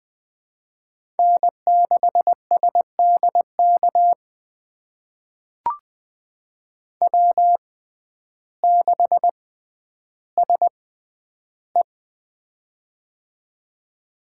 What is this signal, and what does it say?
Morse code, heard as a keyed tone, about 20 wpm, spaced as for 11 wpm, 700 Hz. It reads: N6SDK W6SE
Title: USA Call Signs - 6x Character Spacing - 20wpm